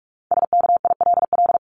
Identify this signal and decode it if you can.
Morse code, heard as a keyed tone, about 44 wpm, 700 Hz. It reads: SKILL
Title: Top 1001-1250 Words 45wpm